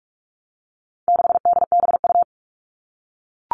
Morse code, heard as a keyed tone, about 44 wpm, 700 Hz. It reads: 6DBU
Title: All Cut Numbers - 4-Digit 45wpm